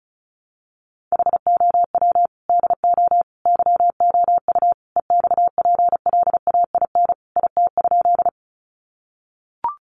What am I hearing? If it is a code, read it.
Morse code, heard as a keyed tone, about 34 wpm, 700 Hz. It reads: HOW DO YOU EXPLAIN IT?
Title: Sentences from Top 1100 Words 35wpm